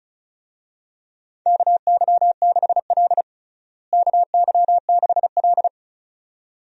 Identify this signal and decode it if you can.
Morse code, heard as a keyed tone, about 34 wpm, 700 Hz. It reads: KY6L KY6L
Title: USA Call Signs - 4 Characters - Repeated 2x - 35wpm